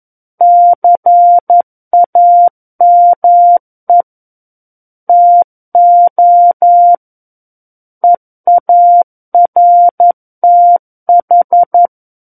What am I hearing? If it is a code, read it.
Morse code, heard as a keyed tone, about 11 wpm, 700 Hz. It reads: CAME TO EARTH
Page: SKCC - War of the Worlds - 02-Chapter 2 - 11 WPM